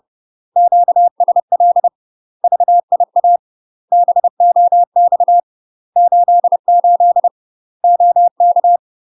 Morse code, heard as a keyed tone, about 30 wpm, 700 Hz. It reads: QSL VIA BOX 88 OK
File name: UK MASTER QSOs TRAINING / 30WPM / UK MASTER QSOs 0026